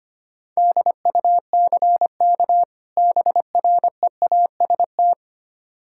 Morse code, heard as a keyed tone, about 25 wpm, 700 Hz. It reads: DUCK BREAST
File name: practice / 25wpm / Difficult Words with Hints - 25wpm